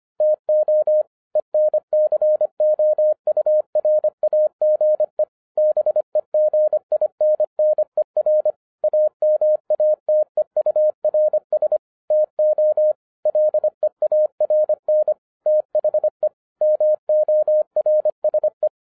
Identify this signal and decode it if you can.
Morse code, heard as a keyed tone, about 25 wpm, 610 Hz. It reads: TO ENCOURAGE BEGINNER AMATEURS TO LEARN THE MORSE